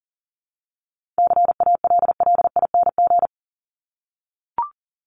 Morse code, heard as a keyed tone, about 40 wpm, 700 Hz. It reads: CALLING
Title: ING Suffix - Top 100 Words with Base Words 40wpm